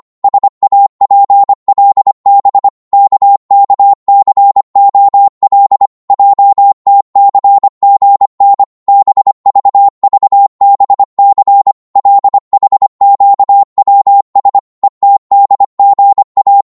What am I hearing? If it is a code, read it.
Morse code, heard as a keyed tone, about 25 wpm, 800 Hz. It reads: SAPLB KKCOL JTCGN BVVBC LHQWS ETDGA
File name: Morse / ltr-25wpm